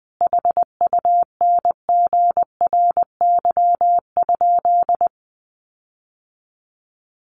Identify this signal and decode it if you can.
Morse code, heard as a keyed tone, about 20 wpm, 700 Hz. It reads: HUNGRY?
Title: Sentences from Top 1400 Words 20wpm